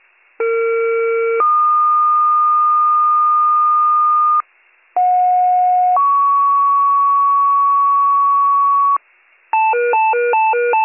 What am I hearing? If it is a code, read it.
Two-tone page, heard as a one-second tone, then a three-second tone, about 460 then 1220 Hz, about 730 then 1120 Hz. It is heard from 0.4 to 9.0 s.